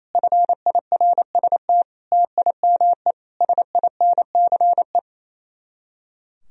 Morse code, heard as a keyed tone, about 34 wpm, 700 Hz. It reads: FIRST TIME SINCE
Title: Speed-Racing - Sets of 4 Words 28wpm